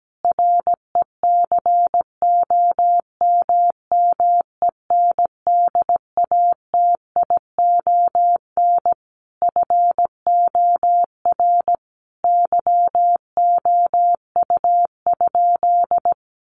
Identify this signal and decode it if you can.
Morse code, heard as a keyed tone, about 17 wpm, 700 Hz. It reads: RECOMMENDATION FOR YOU?
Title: Sentences from Top 1400 Words 17wpm